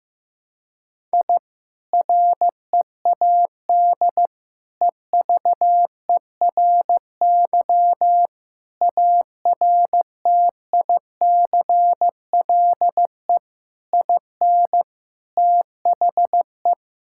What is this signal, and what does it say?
Morse code, heard as a keyed tone, about 15 wpm, 700 Hz. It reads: I READ EVERY ARTICLE IN THE
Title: Sentences from Top 1000 Words - Review Part 5 - 15wpm